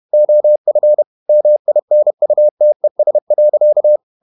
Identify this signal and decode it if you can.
Morse code, heard as a keyed tone, about 31 wpm, 600 Hz. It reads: OF MINUTES.
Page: Never 31wpm | CW med Gnister